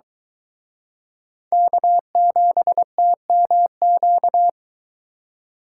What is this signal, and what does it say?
Morse code, heard as a keyed tone, about 23 wpm, 700 Hz. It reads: K7TMQ